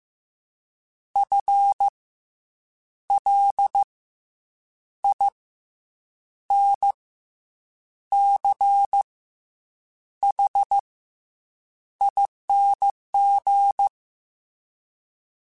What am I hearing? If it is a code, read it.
Morse code, heard as a keyed tone, about 15 wpm, 800 Hz. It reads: F L I N C H ING
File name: Special INGs / Special INGs-mp3s / SlowToFaster ing @13